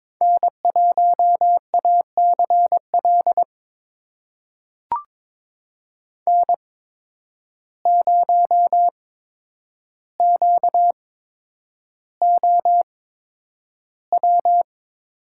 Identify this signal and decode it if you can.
Morse code, heard as a keyed tone, about 22 wpm, spaced as for 9 wpm, 700 Hz. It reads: N1ACL N0QOW
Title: USA Call Signs - 5 Characters - 8x Character Spacing - 22wpm